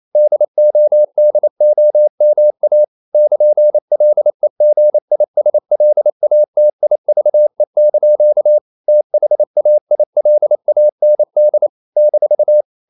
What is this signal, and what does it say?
Morse code, heard as a keyed tone, about 28 wpm, 600 Hz. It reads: DODOMA <KN>LEGISLATIVE) THAILAND -
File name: CW / Practice / NationalCapitals28